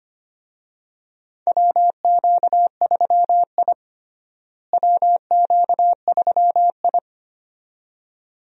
Morse code, heard as a keyed tone, about 25 wpm, 700 Hz. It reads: WQ3I WQ3I
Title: USA Call Signs - 4 Characters - Repeated 2x - 25wpm